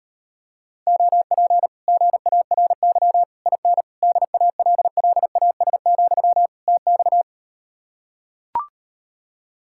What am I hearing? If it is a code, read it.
Morse code, heard as a keyed tone, about 38 wpm, 700 Hz. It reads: OP GARY IN DALLAS, TX